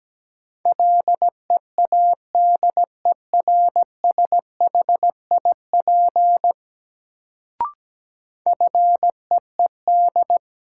Morse code, heard as a keyed tone, about 17 wpm, 700 Hz. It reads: LEADERSHIP FEED
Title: Sentences from Top 1600 Words - New Word Drill - 17wpm